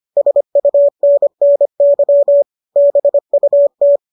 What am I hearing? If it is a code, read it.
Morse code, heard as a keyed tone, about 25 wpm, 570 Hz. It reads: SUNNY BUT